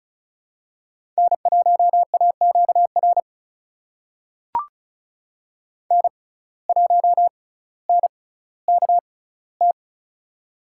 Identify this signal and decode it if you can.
Morse code, heard as a keyed tone, about 36 wpm, spaced as for 18 wpm, 700 Hz. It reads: N1AQR N1NKT
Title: USA Call Signs - 5 Characters - 6x Character Spacing - 35wpm